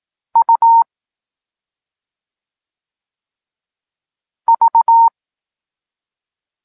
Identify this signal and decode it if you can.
Morse code, heard as a keyed tone, about 18 wpm, spaced as for 2 wpm, 920 Hz. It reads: UV
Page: Tekst 8 – doskonalenie odbioru poznanych liter